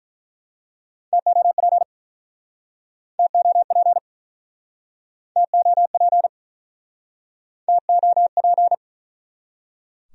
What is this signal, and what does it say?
Morse code, heard as a keyed tone, about 48 wpm, 700 Hz. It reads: TOP TOP TOP TOP